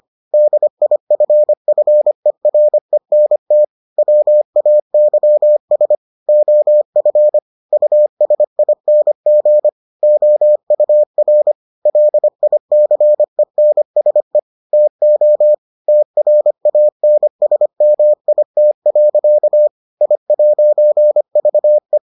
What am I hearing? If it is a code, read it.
Morse code, heard as a keyed tone, about 25 wpm, 600 Hz. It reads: DIFFERENT WAYS OF USING OUR LICENSE TO TRANSMIT. I'VE